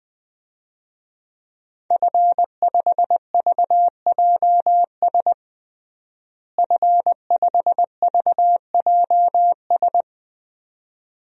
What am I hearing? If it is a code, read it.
Morse code, heard as a keyed tone, about 20 wpm, 700 Hz. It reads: F5VJS F5VJS
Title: DXCC-100 Call Signs - 5 Characters - Repeated 2x - 20wpm